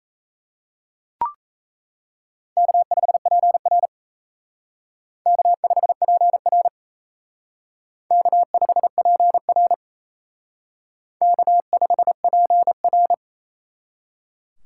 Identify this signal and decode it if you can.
Morse code, heard as a keyed tone, about 34 wpm, 700 Hz. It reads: K5PR K5PR K5PR K5PR
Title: Speed Racing - USA 1x2 Call Signs 28wpm